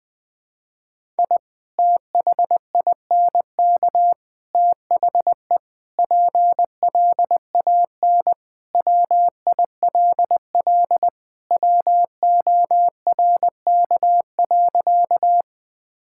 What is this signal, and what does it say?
Morse code, heard as a keyed tone, about 21 wpm, 700 Hz. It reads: I THINK THE PLAN WILL WORK.
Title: Sentences with 3-Word Phrases - 20wpm